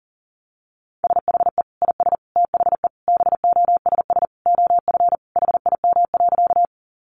Morse code, heard as a keyed tone, about 40 wpm, 700 Hz. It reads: SHE IS THE BOSS OF HIM.
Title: Sentences from Top 700 Words 40wpm